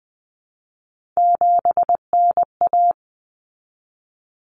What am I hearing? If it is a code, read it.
Morse code, heard as a keyed tone, about 20 wpm, 700 Hz. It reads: 7NA
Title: Common Cut Numbers - 3-Digit 20wpm